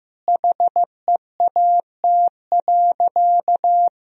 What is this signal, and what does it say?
Morse code, heard as a keyed tone, about 15 wpm, 700 Hz. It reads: HEAT.
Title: Sentences from Top 1000 Words - Review Part 3 - 15wpm